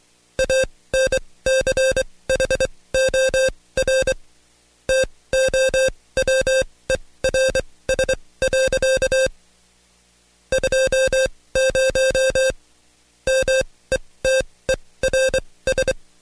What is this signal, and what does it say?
Morse code, heard as a keyed tone, about 24 wpm, spaced as for 18 wpm, 530 Hz. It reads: ANCHOR TOWERS. 20 METERS
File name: Code Quick / AudioDownld2 / Seg135-Stepper8 17WPM